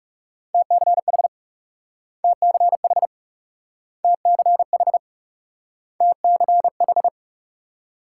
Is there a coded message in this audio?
Yes: Morse code from 0.5 to 7.1 s, heard as a keyed tone, about 38 wpm, 700 Hz. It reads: TCH TCH TCH TCH